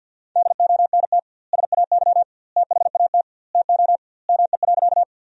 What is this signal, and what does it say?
Morse code, heard as a keyed tone, about 52 wpm, 700 Hz. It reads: DONT SAY THAT TO ME.
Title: 2x Word-Spacing - Sentences from Top 300 Words - 50wpm